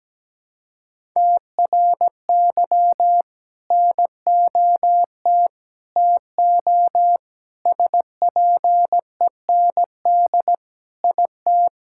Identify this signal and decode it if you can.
Morse code, heard as a keyed tone, about 17 wpm, 700 Hz. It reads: TRY NOT TO SPEND IT
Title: Sentences from Top 1000 Words - Review Part 5 - 17wpm